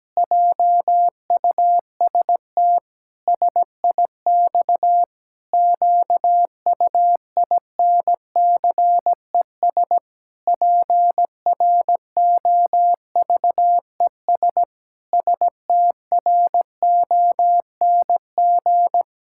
Morse code, heard as a keyed tone, about 17 wpm, 700 Hz. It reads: JUST SIX QUINCES PROVES STRONG
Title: Pangram Challenge v2 - 17wpm